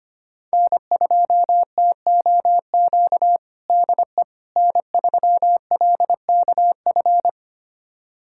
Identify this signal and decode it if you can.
Morse code, heard as a keyed tone, about 25 wpm, 700 Hz. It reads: N2TOQ DE N3LKF
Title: USA Callsign DE Callsign - 5 Characters - 25wpm